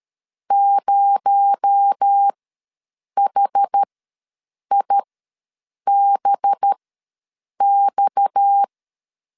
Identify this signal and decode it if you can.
Morse code, heard as a keyed tone, about 13 wpm, spaced as for 7 wpm, 800 Hz. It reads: 0HIBX
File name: LEZIONI / 7 GF9Z / 7 G ws 12wpm 3140